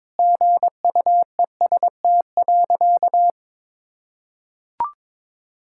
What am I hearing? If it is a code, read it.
Morse code, heard as a keyed tone, about 22 wpm, 700 Hz. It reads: GUEST.
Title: Sentences from Top 1600 Words 22wpm